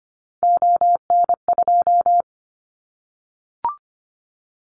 Morse code, heard as a keyed tone, about 25 wpm, 700 Hz. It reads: ON2
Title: Callsign Prefixes for Top 25 DXCC - 25wpm